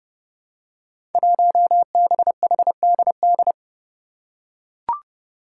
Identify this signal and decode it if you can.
Morse code, heard as a keyed tone, about 30 wpm, 700 Hz. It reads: 1BHDD
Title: Five Challenging Letter-Number Combinations 30wpm